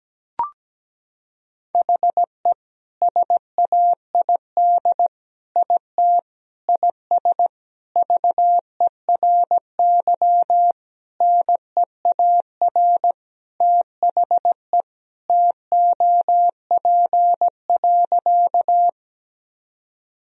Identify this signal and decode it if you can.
Morse code, heard as a keyed tone, about 17 wpm, 700 Hz. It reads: HE SAID IT IS VERY NEAR THE TOP.
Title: Sentences from Top 1000 Words - Review Part 2 - 17wpm